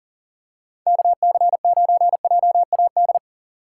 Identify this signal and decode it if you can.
Morse code, heard as a keyed tone, about 40 wpm, 700 Hz. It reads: KC9JAD
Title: USA Call Signs - 6 Characters - 6x Character Spacing - 40wpm